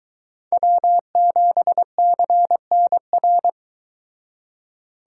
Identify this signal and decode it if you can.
Morse code, heard as a keyed tone, about 23 wpm, 700 Hz. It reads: W7CNR